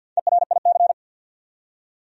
Morse code, heard as a keyed tone, about 53 wpm, 700 Hz. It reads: ERIC
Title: USA Common Names - 50wpm